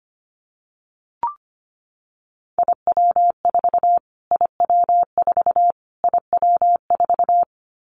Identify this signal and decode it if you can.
Morse code, heard as a keyed tone, about 25 wpm, 700 Hz. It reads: IW4 IW4 IW4
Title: Callsign Prefixes for Top 25 DXCC - Repeated 3x - 25wpm